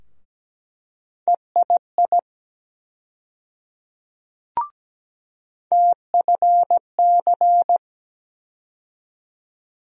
Morse code, heard as a keyed tone, about 17 wpm, 700 Hz. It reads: EII TFC
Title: USA Call Sign Postfixes - 3 Letters 17wpm